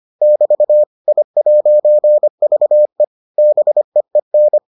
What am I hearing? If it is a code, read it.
Morse code, heard as a keyed tone, about 25 wpm, 600 Hz. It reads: <BT> I'VE BEEN